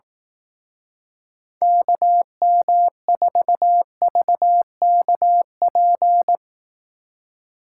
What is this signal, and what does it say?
Morse code, heard as a keyed tone, about 18 wpm, 700 Hz. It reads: KM4VKP
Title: Speed Racing - USA 2x3 Call Signs 15wpm